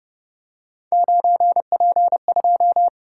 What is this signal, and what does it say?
Morse code, heard as a keyed tone, about 29 wpm, 700 Hz. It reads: 9P2